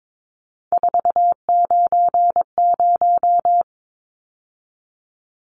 Morse code, heard as a keyed tone, about 22 wpm, 700 Hz. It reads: 490
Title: Numbers - 3-Digit 22wpm